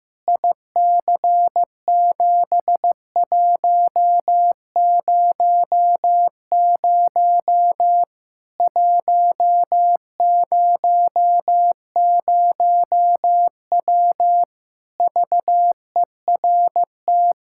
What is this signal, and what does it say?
Morse code, heard as a keyed tone, about 15 wpm, 700 Hz. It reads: IC7100 100W VERT
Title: K1USN SST QSO Practice 15wpm